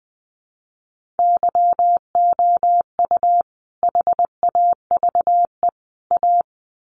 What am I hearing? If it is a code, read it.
Morse code, heard as a keyed tone, about 20 wpm, 700 Hz. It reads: YOU HAVE A
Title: Sets of 3 Words 20wpm